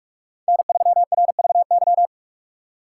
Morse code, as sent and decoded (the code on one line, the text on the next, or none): -. ...-- .-. ...- -.--
N3RVY